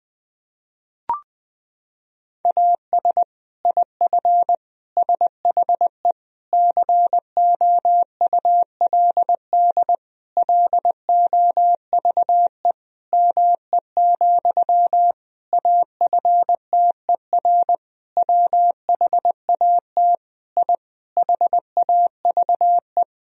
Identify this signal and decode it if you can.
Morse code, heard as a keyed tone, about 21 wpm, 700 Hz. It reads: AS IF SHE COULD LOVE ME, AFTER WHAT I HAVE
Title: Sentences from Top 1000 Words - Review Part 1 - 20wpm